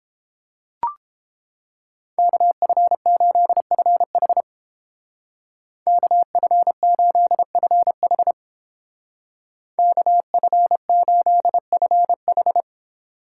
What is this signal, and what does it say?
Morse code, heard as a keyed tone, about 27 wpm, 700 Hz. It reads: KF8FH KF8FH KF8FH
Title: Speed Racing - USA 2x2 Call Signs 22wpm